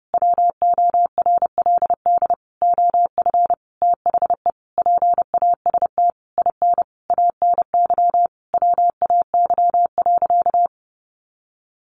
Morse code, heard as a keyed tone, about 30 wpm, 700 Hz. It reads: WORLD OF THE PAST IN ANY WAY.
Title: A Sound Of Thunder 30wpm